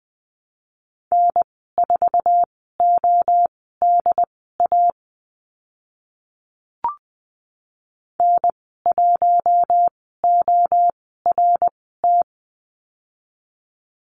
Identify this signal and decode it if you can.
Morse code, heard as a keyed tone, about 20 wpm, spaced as for 14 wpm, 700 Hz. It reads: N4ODA N1ORT
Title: 076 Lesson 13g - Letter L - Call Signs 20wpm